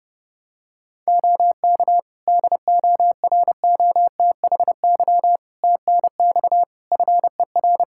Morse code, heard as a keyed tone, about 28 wpm, 700 Hz. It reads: OK DOROTHY TNX FER
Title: Practice QSO - Exchange 4 - Ending - 30wpm